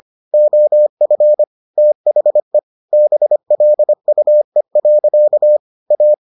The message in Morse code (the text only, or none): OF THE BLUE. A